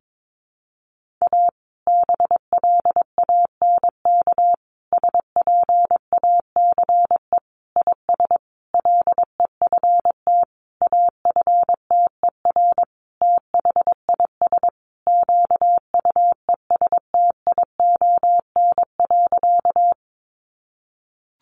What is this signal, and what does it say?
Morse code, heard as a keyed tone, about 22 wpm, 700 Hz. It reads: A BLANK SPACE IS LEFT AFTER THIS QUESTION.
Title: Sentences from Top 1500 Words 22wpm